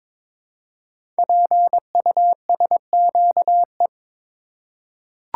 Morse code, heard as a keyed tone, about 23 wpm, 700 Hz. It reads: PUSQE
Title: Code Groups - 5 Letters - 22wpm